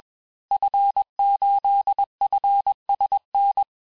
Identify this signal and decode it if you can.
Morse code, heard as a keyed tone, about 21 wpm, 800 Hz. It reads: F8FSN
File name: teste-FIN-de-QSO-vitesse-20mpm